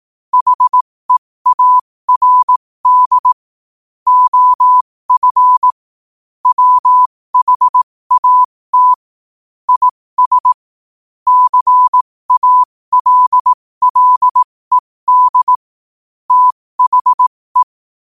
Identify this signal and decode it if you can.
Morse code, heard as a keyed tone, about 18 wpm, spaced as for 15 wpm, 1000 Hz. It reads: HEARD OF WHAT IS CALLED THE
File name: qotdfiles / 15wpm